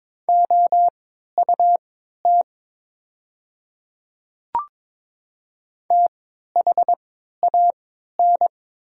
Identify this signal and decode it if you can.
Morse code, heard as a keyed tone, about 23 wpm, spaced as for 13 wpm, 700 Hz. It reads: OUT THAN
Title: 3x Character Spacing - Top 100 Words - 22wpm